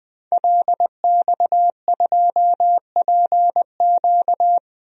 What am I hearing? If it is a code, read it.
Morse code, heard as a keyed tone, about 20 wpm, 700 Hz. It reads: LX2PQ
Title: DXCC-100 Call Signs - 5 Characters - 20wpm